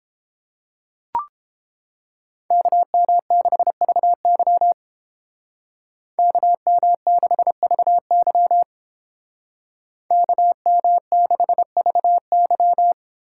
Morse code, heard as a keyed tone, about 29 wpm, 700 Hz. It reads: KM6VY KM6VY KM6VY